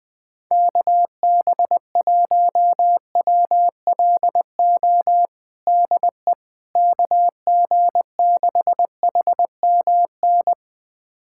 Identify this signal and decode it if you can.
Morse code, heard as a keyed tone, about 20 wpm, 700 Hz. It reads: KB1WLO DE KG6HMN
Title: USA Callsign DE Callsign 20wpm